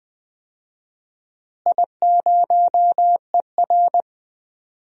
Morse code, heard as a keyed tone, about 20 wpm, 700 Hz. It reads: I0ER
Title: DXCC-100 Call Signs - 4 Characters - Repeated 2x - 20wpm